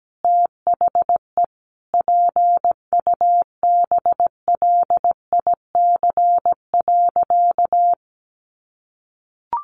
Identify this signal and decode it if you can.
Morse code, heard as a keyed tone, about 17 wpm, 700 Hz. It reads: THE PUBLIC.
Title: Sentences from Top 1500 Words 17wpm